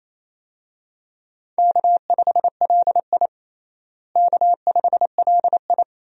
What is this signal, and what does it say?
Morse code, heard as a keyed tone, about 28 wpm, 700 Hz. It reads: K5LI K5LI
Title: USA Call Signs - 4 Characters - Repeated 2x - 28wpm